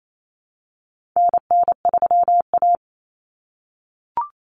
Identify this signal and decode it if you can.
Morse code, heard as a keyed tone, about 28 wpm, 700 Hz. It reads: NN3A
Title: Common Cut Numbers - 4-Digit 28wpm